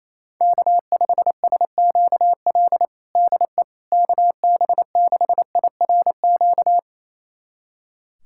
Morse code, heard as a keyed tone, about 28 wpm, 700 Hz. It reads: K5SQL DE KB6IRQ
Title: USA Callsign DE Callsign 28wpm